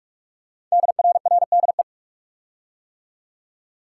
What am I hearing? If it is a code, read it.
Morse code, heard as a keyed tone, about 44 wpm, 700 Hz. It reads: DRRDE
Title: Code Groups - 5 Letters - 45wpm